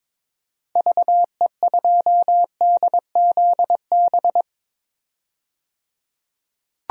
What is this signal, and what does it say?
Morse code, heard as a keyed tone, about 22 wpm, 700 Hz. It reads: VE2DZB
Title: DXCC-100 Call Signs - 6 Characters - 22wpm